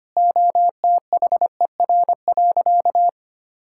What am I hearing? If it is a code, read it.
Morse code, heard as a keyed tone, about 25 wpm, 700 Hz. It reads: OTHER.
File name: practice / 25wpm / Sentences with 2-Word Phrases - 25wpm